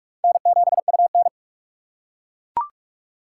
Morse code, heard as a keyed tone, about 41 wpm, 700 Hz. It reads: N7UN
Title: USA Call Signs - 4 Characters - 4x Character Spacing - 45wpm